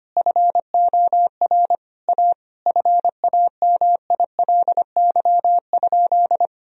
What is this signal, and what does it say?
Morse code, heard as a keyed tone, about 25 wpm, 700 Hz. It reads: FOR A FAMILY?
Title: Sentences from Top 1000 Words - Review Part 4 - 25wpm